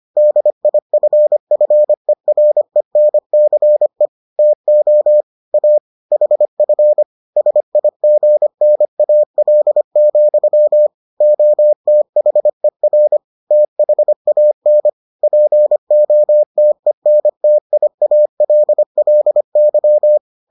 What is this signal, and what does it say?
Morse code, heard as a keyed tone, about 25 wpm, 600 Hz. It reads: DIFFERENCE TO A HF SIGNAL, OTHER THAN POTENTIALLY